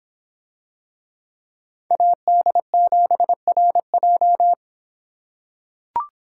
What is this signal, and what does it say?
Morse code, heard as a keyed tone, about 25 wpm, 700 Hz. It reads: AD7RJ